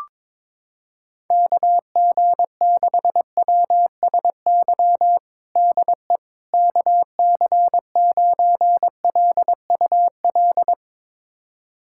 Morse code, heard as a keyed tone, about 22 wpm, 700 Hz. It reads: KG6WSY DE KC9LUL
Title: USA Callsign DE Callsign - 6 Characters - 22wpm